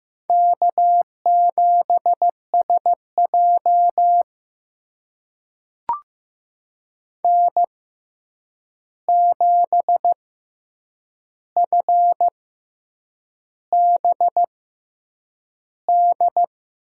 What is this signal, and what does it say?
Morse code, heard as a keyed tone, about 15 wpm, spaced as for 7 wpm, 700 Hz. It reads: K7SJ N7FBD
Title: USA Call Signs - 6x Character Spacing - 15wpm